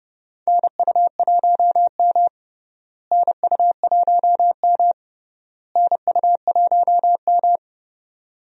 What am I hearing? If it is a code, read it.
Morse code, heard as a keyed tone, about 30 wpm, 700 Hz. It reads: NU1M NU1M NU1M
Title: USA Call Signs - 4 Characters - Repeated 3x - 30wpm